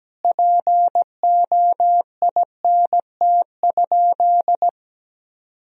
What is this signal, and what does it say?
Morse code, heard as a keyed tone, about 17 wpm, 700 Hz. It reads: POINT?
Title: Sentences from Top 1000 Words - Review Part 1 - 17wpm